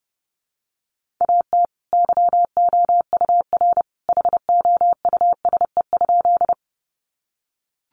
Morse code, heard as a keyed tone, about 30 wpm, 700 Hz. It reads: AT YOUR HOUSE?
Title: Sentences from Top 200 Words 30wpm